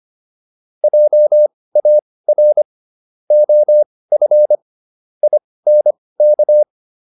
Morse code, heard as a keyed tone, about 25 wpm, spaced as for 18 wpm, 600 Hz. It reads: JAR OF INK
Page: intermediate-practice-files